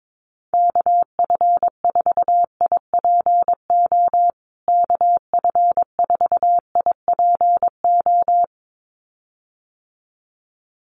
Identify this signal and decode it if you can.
Morse code, heard as a keyed tone, about 22 wpm, 700 Hz. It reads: KF4IPO KF4IPO
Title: USA Call Signs Repeated 2x 22wpm